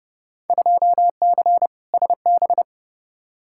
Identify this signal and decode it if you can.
Morse code, heard as a keyed tone, about 30 wpm, 700 Hz. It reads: 2C SB
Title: Field Day Exchanges Encore - 30wpm